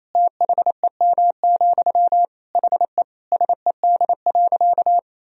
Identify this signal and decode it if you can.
Morse code, heard as a keyed tone, about 27 wpm, 700 Hz. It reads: THEM, HE SED.
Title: Sentences from Top 400 Words and Abbreviations 28wpm